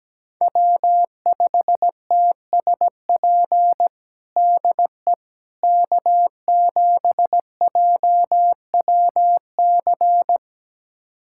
Morse code, heard as a keyed tone, about 17 wpm, 700 Hz. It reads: W5TSP DE K7JWC
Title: USA Callsign DE Callsign - 5 Characters - 17wpm